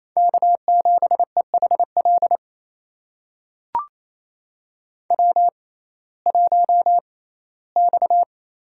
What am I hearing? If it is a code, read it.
Morse code, heard as a keyed tone, about 28 wpm, spaced as for 17 wpm, 700 Hz. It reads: K7EHL W1X
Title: USA Call Signs - 5 Characters - 6x Character Spacing - 28wpm